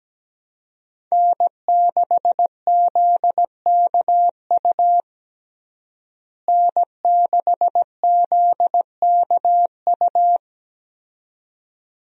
Morse code, heard as a keyed tone, about 17 wpm, 700 Hz. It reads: N6ZKU N6ZKU
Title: USA Call Signs - 5 Characters - Repeated 2x - 17wpm